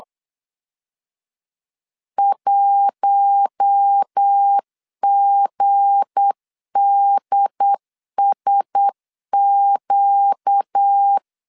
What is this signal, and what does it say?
Morse code, heard as a keyed tone, about 8 wpm, 800 Hz. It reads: 1GDSQ
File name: LEZIONI / 7 GF9Z / 7 GF9Z 8wpm 3213